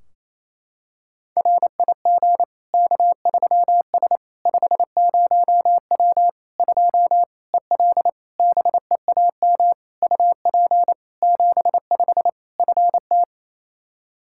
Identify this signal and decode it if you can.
Morse code, heard as a keyed tone, about 28 wpm, 700 Hz. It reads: RIG K3S 50W 2 EL BEAM UP 75 FT
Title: Practice QSO - Exchange 2 - Info - 28wpm